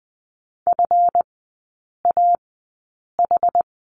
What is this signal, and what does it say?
Morse code, heard as a keyed tone, about 20 wpm, spaced as for 8 wpm, 700 Hz. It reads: FAH
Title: 152 Lesson 25e - Letter p - Three Characters 20wpm